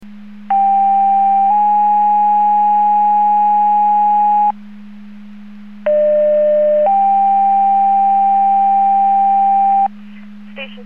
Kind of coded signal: two-tone page